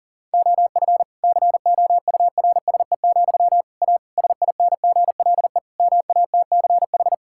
Morse code, heard as a keyed tone, about 37 wpm, 700 Hz. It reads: OF COURSE, A SINGLE MATCH
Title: Sentences from Top 1600 Words 40wpm